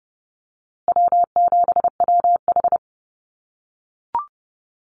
Morse code, heard as a keyed tone, about 30 wpm, 700 Hz. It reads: W7WH
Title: USA Call Signs Repeated 2x 30wpm